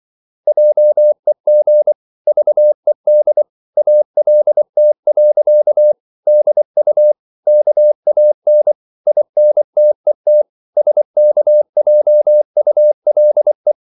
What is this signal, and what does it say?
Morse code, heard as a keyed tone, about 24 wpm, 600 Hz. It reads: JEG VED ALT. DU KAN INTET SKJULE
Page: Ferie 24 wpm